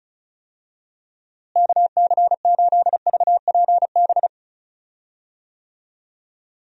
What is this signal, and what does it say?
Morse code, heard as a keyed tone, about 36 wpm, 700 Hz. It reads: KC8VPB
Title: USA Call Signs - 6 Characters - Repeated 2x - 35wpm